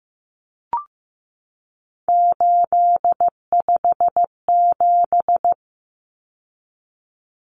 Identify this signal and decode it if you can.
Morse code, heard as a keyed tone, about 15 wpm, 700 Hz. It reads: 857
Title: Numbers - 3-Digit 15wpm